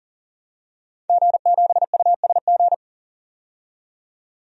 Morse code, heard as a keyed tone, about 40 wpm, 700 Hz. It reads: G7USG
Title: DXCC-100 Call Signs - 5 Characters - 40wpm